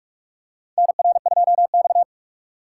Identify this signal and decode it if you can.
Morse code, heard as a keyed tone, about 46 wpm, 700 Hz. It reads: NR2X
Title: USA Call Signs - 4 Characters - 2x Character Spacing - 45wpm